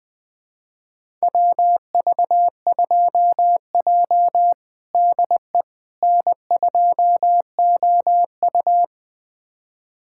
Morse code, heard as a keyed tone, about 20 wpm, 700 Hz. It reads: WV2J DE N2OU
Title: USA Callsign DE Callsign - 4 Characters - 20wpm